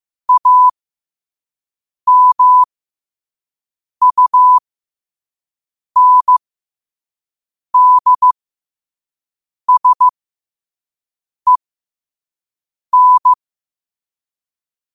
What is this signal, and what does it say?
Morse code, heard as a keyed tone, about 15 wpm, spaced as for 5 wpm, 1000 Hz. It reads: AMUNDSEN
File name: qotdfiles / 5wpm